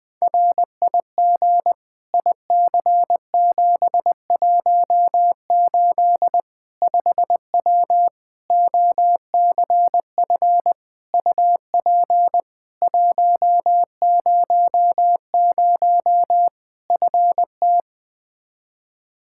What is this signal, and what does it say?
Morse code, heard as a keyed tone, about 20 wpm, 700 Hz. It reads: RIG IC718 5W OCF UP 100 FT
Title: K1USN SST QSO Practice 20wpm